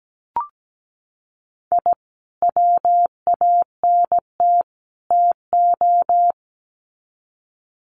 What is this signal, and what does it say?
Morse code, heard as a keyed tone, about 17 wpm, 700 Hz. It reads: I WANT TO
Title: Sets of 3 Words 17wpm